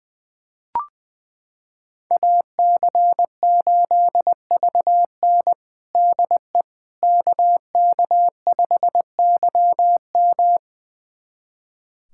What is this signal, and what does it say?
Morse code, heard as a keyed tone, about 20 wpm, 700 Hz. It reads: AC8VN DE KK5YM
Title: USA Callsign DE Callsign - 5 Characters - 20wpm